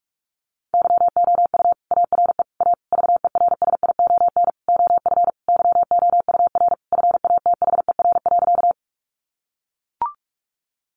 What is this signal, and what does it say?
Morse code, heard as a keyed tone, about 44 wpm, 700 Hz. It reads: YOU ARE A VERSION OF YOUR FATHER.
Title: Sentences from Top 1400 Words 45wpm